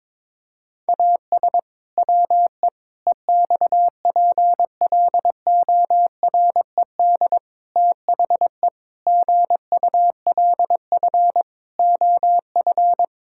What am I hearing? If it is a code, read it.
Morse code, heard as a keyed tone, about 22 wpm, 700 Hz. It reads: AS WE EXPLORED THE GULF OF
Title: Pangram Challenge v2 - 22wpm